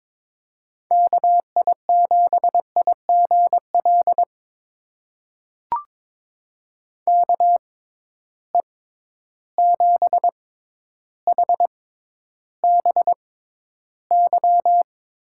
Morse code, heard as a keyed tone, about 23 wpm, spaced as for 11 wpm, 700 Hz. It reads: KI7IGL KE7HBY
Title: USA Call Signs - 6 Characters - 6x Character Spacing - 22wpm